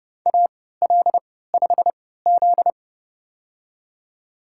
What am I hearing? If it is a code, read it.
Morse code, heard as a keyed tone, about 30 wpm, spaced as for 17 wpm, 700 Hz. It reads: AL5Z
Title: USA Call Signs - 4 Characters - 3x Character Spacing - 30wpm